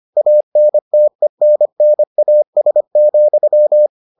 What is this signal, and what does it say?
Morse code, heard as a keyed tone, about 25 wpm, 600 Hz. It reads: ANTENNAS,